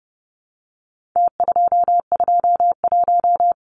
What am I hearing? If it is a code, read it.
Morse code, heard as a keyed tone, about 30 wpm, 700 Hz. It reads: T221
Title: Numbers - Contest Style - 4-Digit 30wpm